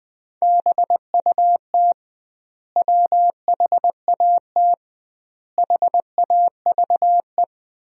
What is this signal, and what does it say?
Morse code, heard as a keyed tone, about 20 wpm, 700 Hz. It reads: BUT WHAT HAVE
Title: 2x Word-Spacing - Sentences from Top 100 Words - 20wpm